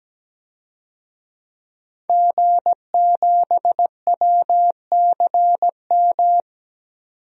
Morse code, heard as a keyed tone, about 17 wpm, 700 Hz. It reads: G7WCM
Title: DXCC-100 Call Signs - 5 Characters - 17wpm